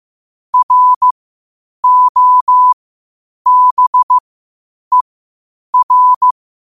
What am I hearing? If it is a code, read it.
Morse code, heard as a keyed tone, about 15 wpm, spaced as for 8 wpm, 1000 Hz. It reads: ROBER
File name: qotdfiles / 8wpm